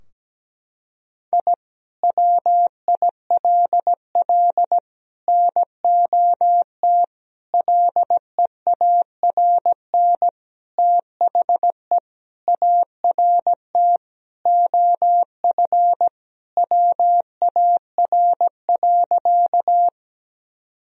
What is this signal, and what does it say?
Morse code, heard as a keyed tone, about 17 wpm, 700 Hz. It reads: I WILL NOT LEARN THE ART OF WAR.
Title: Sentences from Top 1000 Words - Review Part 3 - 17wpm